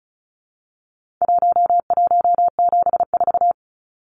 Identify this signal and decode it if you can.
Morse code, heard as a keyed tone, about 34 wpm, 700 Hz. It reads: 1174
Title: Numbers - Contest Style - 4-Digit 35wpm